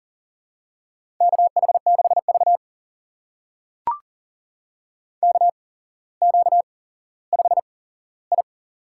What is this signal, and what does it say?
Morse code, heard as a keyed tone, about 40 wpm, spaced as for 17 wpm, 700 Hz. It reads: KH6V KQ5I
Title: USA Call Signs - 8x Character Spacing - 40wpm